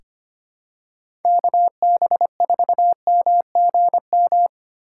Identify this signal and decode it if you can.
Morse code, heard as a keyed tone, about 25 wpm, 700 Hz. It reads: KB4MGM
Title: USA Call Signs - 6 Characters - 6x Character Spacing - 25wpm